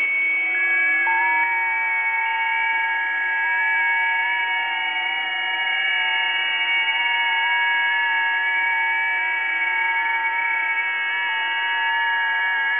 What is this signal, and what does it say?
FT8.
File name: tests / websdr test11